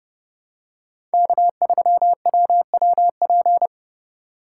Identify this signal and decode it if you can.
Morse code, heard as a keyed tone, about 30 wpm, 700 Hz. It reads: K3WWP
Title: K1USN SST QSO Practice - 30wpm